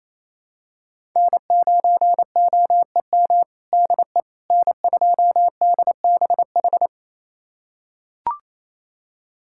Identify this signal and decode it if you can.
Morse code, heard as a keyed tone, about 28 wpm, 700 Hz. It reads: N9OEM DE N2DBH
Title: USA Callsign DE Callsign - 5 Characters - 28wpm